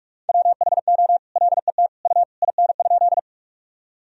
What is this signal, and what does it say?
Morse code, heard as a keyed tone, about 46 wpm, 700 Hz. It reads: WHO LET U IN?
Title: Sentences from Top 300 Words and Abbreviations 45wpm